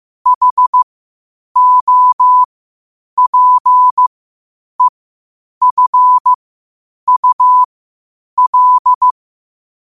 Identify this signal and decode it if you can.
Morse code, heard as a keyed tone, about 15 wpm, spaced as for 8 wpm, 1000 Hz. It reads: HOPEFUL
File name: qotdfiles / 8wpm